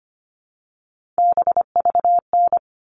Morse code, heard as a keyed tone, about 25 wpm, 700 Hz. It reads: BVN
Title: All Cut Numbers - 3-Digit 25wpm